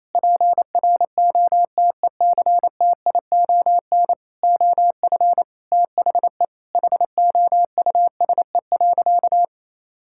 Morse code, heard as a keyed tone, about 28 wpm, 700 Hz. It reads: PROTECTION OF THE HOUSE.
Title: Sentences from Top 1600 Words 28wpm